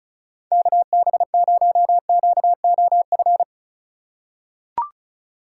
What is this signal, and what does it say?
Morse code, heard as a keyed tone, about 36 wpm, 700 Hz. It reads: KB0QOF
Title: USA Call Signs - 6 Characters - 3x Character Spacing - 35wpm